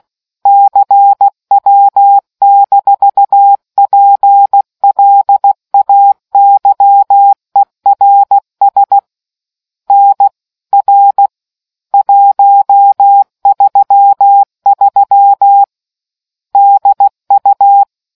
Morse code, heard as a keyed tone, about 16 wpm, 800 Hz. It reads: CW-PLAYERS NR 133 DU